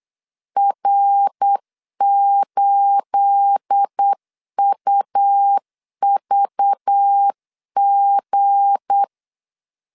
Morse code, heard as a keyed tone, about 8 wpm, 800 Hz. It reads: R8UVG
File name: LEZIONI / 9 28 8wpm 3512